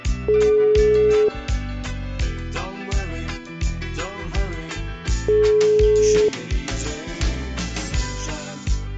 Ringback tone (Europe, 425 Hz).